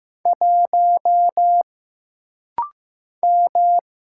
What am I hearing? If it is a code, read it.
Morse code, heard as a keyed tone, about 15 wpm, 700 Hz. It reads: 1 M
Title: Single Letter-Number Alt - Rapid-Fire 15wpm